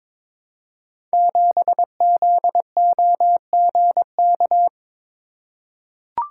Morse code, heard as a keyed tone, about 22 wpm, 700 Hz. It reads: 7ZOGK
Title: Code Groups - 5 Letter:Numbers - 22wpm